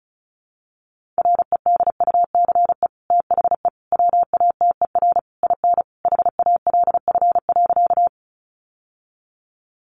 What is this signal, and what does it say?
Morse code, heard as a keyed tone, about 34 wpm, 700 Hz. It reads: REDUCE THE WATER IN HALF.
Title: Sentences from Top 1100 Words 35wpm